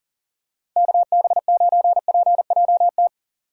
Morse code, heard as a keyed tone, about 40 wpm, 700 Hz. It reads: KB9PJT
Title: USA Call Signs - 6 Characters - 2x Character Spacing - 40wpm